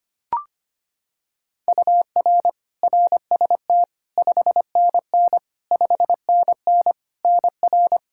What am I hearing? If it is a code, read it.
Morse code, heard as a keyed tone, about 25 wpm, 700 Hz. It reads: UR RST 5NN 5NN NR
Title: K1USN SST QSO Practice 25wpm